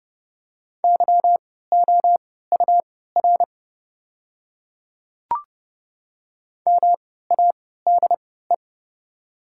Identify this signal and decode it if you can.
Morse code, heard as a keyed tone, about 30 wpm, spaced as for 17 wpm, 690 Hz. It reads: YOUR MADE
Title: 3x Character Spacing - Top 200 Words - 30wpm